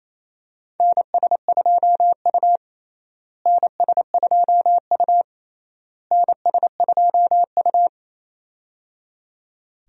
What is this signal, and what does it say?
Morse code, heard as a keyed tone, about 28 wpm, 700 Hz. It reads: NS2U NS2U NS2U
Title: USA Call Signs - 4 Characters - Repeated 3x - 28wpm